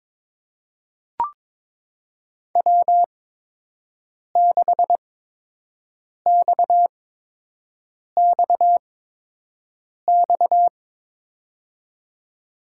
Morse code, heard as a keyed tone, about 21 wpm, spaced as for 6 wpm, 700 Hz. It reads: W6XXX
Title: USA Call Signs - 8x Character Spacing - 22wpm